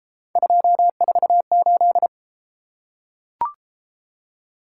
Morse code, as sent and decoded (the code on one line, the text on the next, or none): ..--- ....- ---..
248